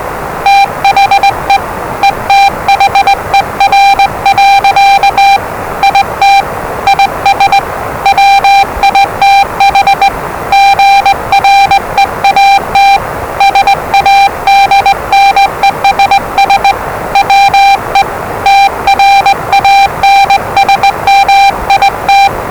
Morse code, heard as a keyed tone, about 20 wpm, 790 Hz, with noise. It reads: THE ETHER. IT IS WITH GREAT SADNESS WE TRANSMIT
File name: WellingtonRadioZlwCloseDownOnHfAndMf / 01-Zlw 4277khz 417.5 qrt until 0925z